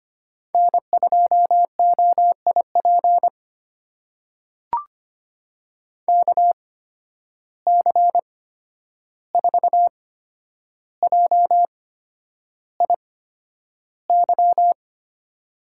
Morse code, heard as a keyed tone, about 25 wpm, spaced as for 10 wpm, 700 Hz. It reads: N2OIP KC4JIY
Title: USA Call Signs - 8x Character Spacing - 25wpm